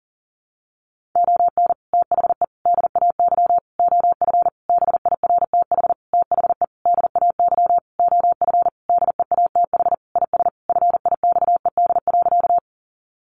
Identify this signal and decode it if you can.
Morse code, heard as a keyed tone, about 37 wpm, 700 Hz. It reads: ON THE DAY OF BIRTH THE DAY OF DEATH IS FIXED.
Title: Sentences from Top 1100 Words 40wpm